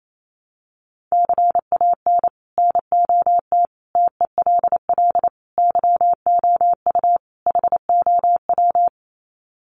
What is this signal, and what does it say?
Morse code, heard as a keyed tone, about 28 wpm, 700 Hz. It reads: CAN NOT TELL YOU HOW
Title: Sets of 5 Words Encore2 28wpm